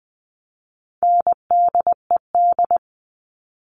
Morse code, heard as a keyed tone, about 20 wpm, 700 Hz. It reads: NDED
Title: All Cut Numbers - 4-Digit 20wpm